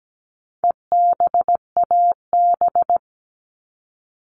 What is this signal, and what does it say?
Morse code, heard as a keyed tone, about 17 wpm, 700 Hz. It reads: EBAB
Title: All Cut Numbers - 4-Digit 17wpm